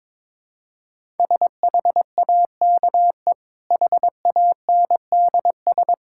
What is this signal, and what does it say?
Morse code, heard as a keyed tone, about 22 wpm, 690 Hz. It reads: SHAKE HANDS
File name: practice / 22wpm / Easy Sentences with Hints - 22wpm